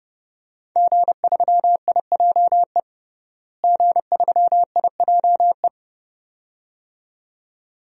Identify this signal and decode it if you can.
Morse code, heard as a keyed tone, about 30 wpm, 700 Hz. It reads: G3IJE G3IJE
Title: DXCC-100 Call Signs - 5 Characters - Repeated 2x - 30wpm